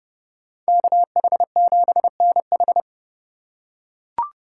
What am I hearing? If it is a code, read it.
Morse code, heard as a keyed tone, about 30 wpm, 700 Hz. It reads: KH7NH
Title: USA Call Signs - 5 Characters - 3x Character Spacing - 30wpm